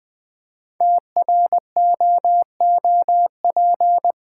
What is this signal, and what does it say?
Morse code, heard as a keyed tone, about 20 wpm, 690 Hz. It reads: TROOP
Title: 5 Letter Words - First 3 Letters Given - 20wpm